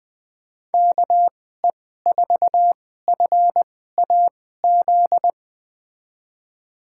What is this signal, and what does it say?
Morse code, heard as a keyed tone, about 20 wpm, spaced as for 14 wpm, 700 Hz. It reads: KE4FAZ
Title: USA Call Signs - 6 Characters - 2x Character Spacing - 20wpm